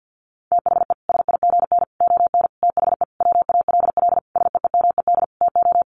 Morse code, heard as a keyed tone, about 50 wpm, 700 Hz. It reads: THE SIGN ON THE WALL SEEMED TO
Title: A Sound Of Thunder 50wpm